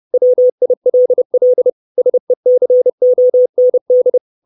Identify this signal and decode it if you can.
Morse code, heard as a keyed tone, about 30 wpm, 500 Hz. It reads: WILL SECOND
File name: 632 Words 30WPM